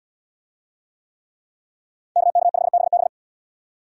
Morse code, heard as a keyed tone, about 77 wpm, 700 Hz. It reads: KK5BB